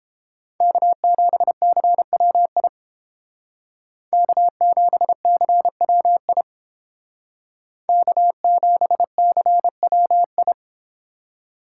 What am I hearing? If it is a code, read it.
Morse code, heard as a keyed tone, about 31 wpm, 700 Hz. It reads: K7CWI K7CWI K7CWI